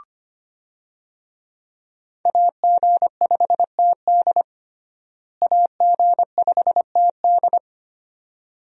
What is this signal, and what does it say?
Morse code, heard as a keyed tone, about 25 wpm, 690 Hz. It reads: AG5TD AG5TD
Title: USA Call Signs - 5 Characters - Repeated 2x - 25wpm